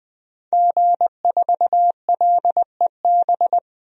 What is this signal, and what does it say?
Morse code, heard as a keyed tone, about 20 wpm, 700 Hz. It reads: G4LEB
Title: DXCC-100 Call Signs - 5 Characters - Repeated 3x - 20wpm